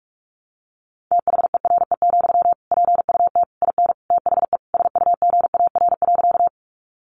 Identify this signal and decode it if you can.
Morse code, heard as a keyed tone, about 44 wpm, 700 Hz. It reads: THERE, PUT IN THE SUGAR.
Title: Sentences from Top 1200 Words 45wpm